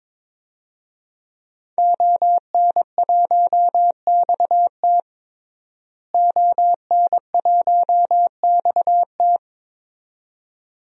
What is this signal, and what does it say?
Morse code, heard as a keyed tone, about 22 wpm, 700 Hz. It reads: ON1XT ON1XT
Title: DXCC-100 Call Signs - 5 Characters - Repeated 2x - 22wpm